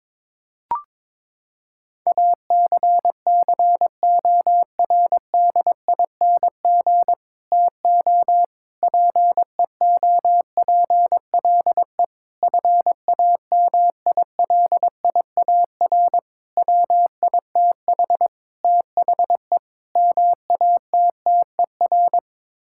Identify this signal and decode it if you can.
Morse code, heard as a keyed tone, about 21 wpm, 700 Hz. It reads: ACCORDING TO PEOPLE FAMILIAR WITH THE MATTER
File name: practice / 22wpm / Sets of 7 Words - 22wpm